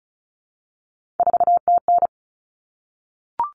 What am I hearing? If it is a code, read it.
Morse code, heard as a keyed tone, about 34 wpm, 700 Hz. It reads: 4TN
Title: Common Cut Numbers - 3-Digit 35wpm